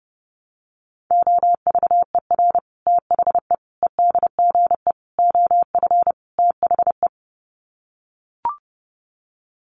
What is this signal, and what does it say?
Morse code, heard as a keyed tone, about 30 wpm, 700 Hz. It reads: OVER THE EDGE OF THE
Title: Sets of 5 Words Encore2 30wpm